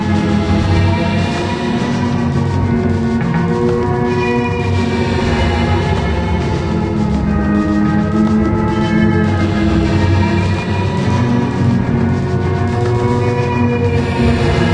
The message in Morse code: none